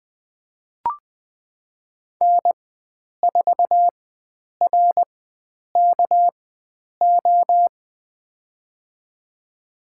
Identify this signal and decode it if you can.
Morse code, heard as a keyed tone, about 20 wpm, spaced as for 9 wpm, 700 Hz. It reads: N4RKO
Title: USA Call Signs - 5 Characters - 4x Character Spacing - 20wpm